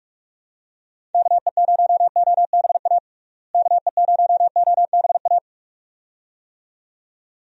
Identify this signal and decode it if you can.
Morse code, heard as a keyed tone, about 46 wpm, 700 Hz. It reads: KE0OBA KE0OBA
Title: USA Call Signs - 6 Characters - Repeated 2x - 45wpm